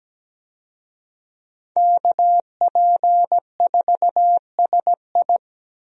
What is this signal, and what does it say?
Morse code, heard as a keyed tone, about 17 wpm, 700 Hz. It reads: KP4SI
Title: USA Call Signs - 5 Characters - 17wpm